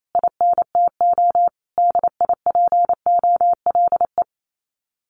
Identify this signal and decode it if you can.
Morse code, heard as a keyed tone, about 28 wpm, 700 Hz. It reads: INTO DIPOLE
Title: Common QSO Phrases Encore 28wpm